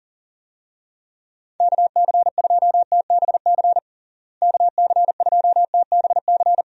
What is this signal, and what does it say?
Morse code, heard as a keyed tone, about 40 wpm, 700 Hz. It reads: KC2TBC KC2TBC
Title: USA Call Signs - 6 Characters - Repeated 3x - 40wpm